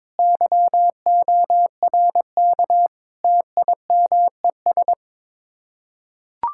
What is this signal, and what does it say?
Morse code, heard as a keyed tone, about 23 wpm, 700 Hz. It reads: YORK TIMES
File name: practice / 22wpm / Build A Phrase - 5-Word Phrases - 22wpm